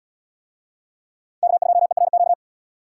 Morse code, heard as a keyed tone, about 78 wpm, 700 Hz. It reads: K2ERQ